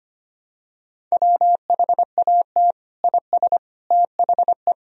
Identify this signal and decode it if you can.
Morse code, heard as a keyed tone, about 25 wpm, 700 Hz. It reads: WHAT IS THE
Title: Sentences from Top 1000 Words - Review Part 1 - 25wpm